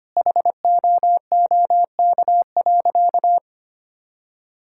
Morse code, heard as a keyed tone, about 25 wpm, 700 Hz. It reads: HOOK.
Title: Sentences from Top 1600 Words 25wpm